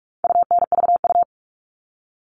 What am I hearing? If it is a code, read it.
Morse code, heard as a keyed tone, about 44 wpm, 700 Hz. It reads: UNVU
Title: All Cut Numbers - 4-Digit 45wpm